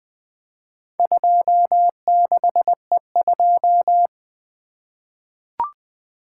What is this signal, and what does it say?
Morse code, heard as a keyed tone, about 21 wpm, 700 Hz. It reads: 26E2
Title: Four Letter-Number Combinations - 20wpm